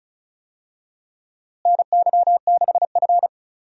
Morse code, heard as a keyed tone, about 36 wpm, 700 Hz. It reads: NY6F
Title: USA Call Signs - 4 Characters - Repeated 3x - 35wpm